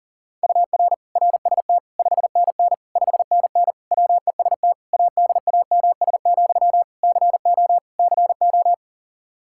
Morse code, heard as a keyed tone, about 40 wpm, 700 Hz. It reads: UR RST 5NN 5NN WEST ADAMS, CO CO
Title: K1USN SST QSO Practice 40wpm